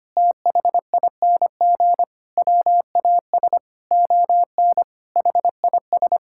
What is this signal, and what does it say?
Morse code, heard as a keyed tone, about 25 wpm, 700 Hz. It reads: THING WAS ON HIS
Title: Sentences from Top 1000 Words - Review Part 5 - 25wpm